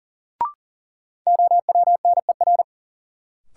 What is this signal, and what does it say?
Morse code, heard as a keyed tone, about 38 wpm, 700 Hz. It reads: OWNER
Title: Sentences from Top 1600 Words - New Word Drill - 40wpm